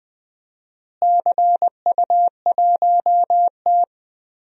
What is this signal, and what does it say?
Morse code, heard as a keyed tone, about 20 wpm, 700 Hz. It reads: CU1T
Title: DXCC-100 Call Signs - 4 Characters - Repeated 3x - 20wpm